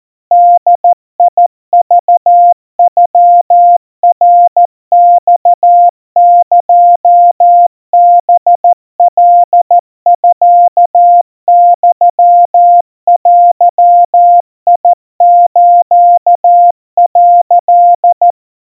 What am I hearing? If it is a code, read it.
Morse code (Japanese Wabun code), heard as a keyed tone, about 14 wpm, 700 Hz. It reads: ボクノナマエハカミユデス」
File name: kawa / Morse sample / kamyu 50 cpm 700 Hz Crispy